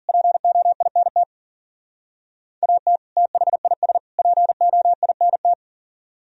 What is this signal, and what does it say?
Morse code, heard as a keyed tone, about 51 wpm, 700 Hz. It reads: POINT AT THIS POINT